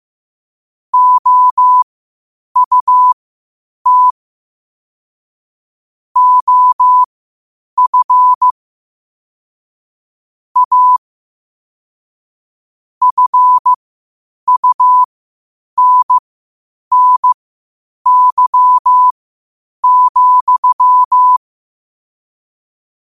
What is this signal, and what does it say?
Morse code, heard as a keyed tone, about 15 wpm, spaced as for 8 wpm, 1000 Hz. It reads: OUT OF A FUNNY,
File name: qotdfiles / 8wpm